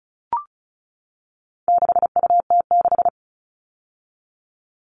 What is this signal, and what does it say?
Morse code, heard as a keyed tone, about 34 wpm, 700 Hz. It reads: 6UT6
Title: All Cut Numbers - 4-Digit 35wpm